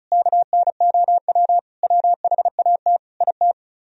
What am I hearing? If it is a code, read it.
Morse code, heard as a keyed tone, about 46 wpm, spaced as for 41 wpm, 700 Hz. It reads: KNOW WHAT IT
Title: Speed-Racing - Sets of 5 Words 35wpm